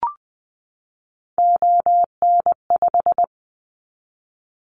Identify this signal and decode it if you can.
Morse code, heard as a keyed tone, about 20 wpm, 700 Hz. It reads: ON5
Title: Callsign Prefixes for Top 100 DXCC - 20wpm